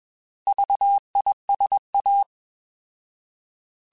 Morse code, heard as a keyed tone, about 21 wpm, 800 Hz. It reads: VISA
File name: Alphabetical / Alphabetical mp3s / Vs @20